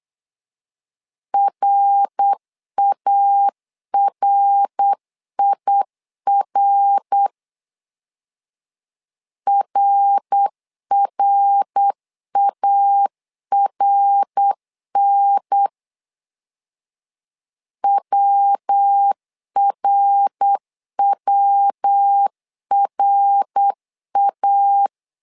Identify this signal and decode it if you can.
Morse code, heard as a keyed tone, about 8 wpm, 800 Hz. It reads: RARIR RRARN WRWRA
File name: LEZIONI / 3 MWRO / 3 MWR 8wpm 2444